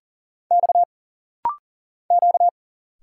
Morse code, heard as a keyed tone, about 40 wpm, 700 Hz. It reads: X Q
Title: Single Letters - Rapid-Fire 40wpm